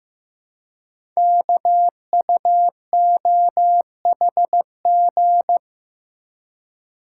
Morse code, heard as a keyed tone, about 15 wpm, 690 Hz. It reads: KUOHG
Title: Code Groups - 5 Letters - 15wpm